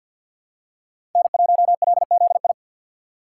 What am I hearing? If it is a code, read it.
Morse code, heard as a keyed tone, about 52 wpm, 700 Hz. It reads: N1LZI
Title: USA Call Signs - 5 Characters - 2x Character Spacing - 50wpm